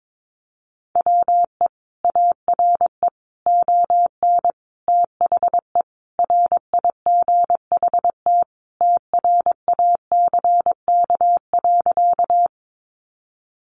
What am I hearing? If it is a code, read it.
Morse code, heard as a keyed tone, about 22 wpm, 700 Hz. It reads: WE ARE ON THE RIGHT TRACK.
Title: Sentences from Top 1100 Words 22wpm